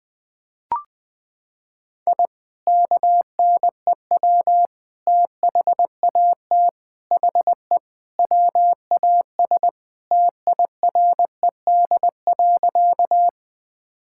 Morse code, heard as a keyed tone, about 20 wpm, 700 Hz. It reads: I KNEW THAT HE WAS TIRED.
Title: Sentences with 3-Word Phrases - 20wpm